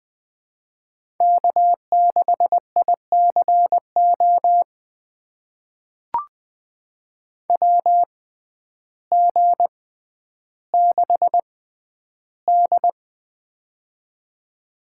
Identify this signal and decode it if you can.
Morse code, heard as a keyed tone, about 20 wpm, spaced as for 11 wpm, 700 Hz. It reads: K6ICO WG6D
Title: USA Call Signs - 6x Character Spacing - 20wpm